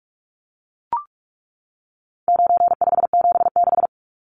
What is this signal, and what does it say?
Morse code, heard as a keyed tone, about 44 wpm, 700 Hz. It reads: 9576
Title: Numbers - 4-Digit 45wpm